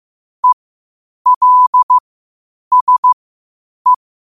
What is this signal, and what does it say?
Morse code, heard as a keyed tone, about 15 wpm, spaced as for 8 wpm, 1000 Hz. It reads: ELSE
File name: qotdfiles / 8wpm